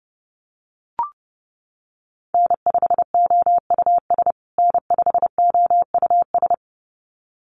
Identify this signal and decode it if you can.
Morse code, heard as a keyed tone, about 30 wpm, 700 Hz. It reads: N5OUS N5OUS
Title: USA Call Signs Repeated 2x 30wpm